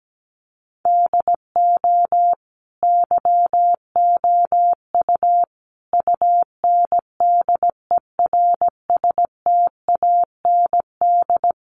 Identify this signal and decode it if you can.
Morse code, heard as a keyed tone, about 17 wpm, 700 Hz. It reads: DO YOU UNDERSTAND
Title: Sentences from Top 1200 Words 17wpm